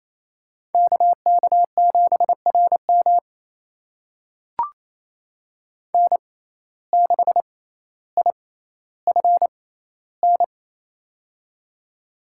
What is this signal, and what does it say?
Morse code, heard as a keyed tone, about 28 wpm, spaced as for 14 wpm, 700 Hz. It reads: KK7RM N6IFN
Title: USA Call Signs - 5 Characters - 6x Character Spacing - 28wpm